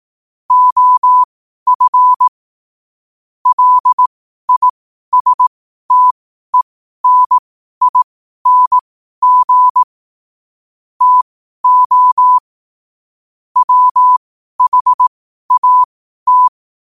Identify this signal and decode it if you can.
Morse code, heard as a keyed tone, about 18 wpm, spaced as for 12 wpm, 1000 Hz. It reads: OF LISTENING TO WHAT
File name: qotdfiles / 12wpm